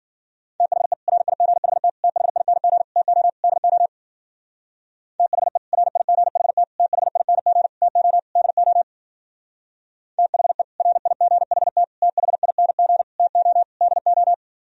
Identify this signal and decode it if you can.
Morse code, heard as a keyed tone, about 50 wpm, 700 Hz. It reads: THE RIGHT THING TO DO THE RIGHT THING TO DO THE RIGHT THING TO DO
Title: Speed-Racing - Sets of 5 Words 40wpm